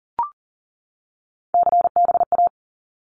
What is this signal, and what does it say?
Morse code, heard as a keyed tone, about 38 wpm, 700 Hz. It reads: CBA
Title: Abbreviations 40wpm